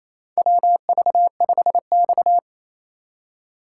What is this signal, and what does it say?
Morse code, heard as a keyed tone, about 28 wpm, 700 Hz. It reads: WV5X
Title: USA Call Signs - 4 Characters - 3x Character Spacing - 28wpm